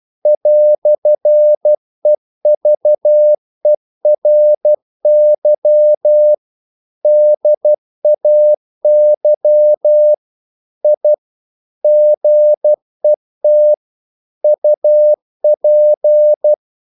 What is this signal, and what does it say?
Morse code, heard as a keyed tone, about 12 wpm, 600 Hz. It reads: "EVERY DAY I GET UP
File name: CW / Practice / EnglishQuotations0312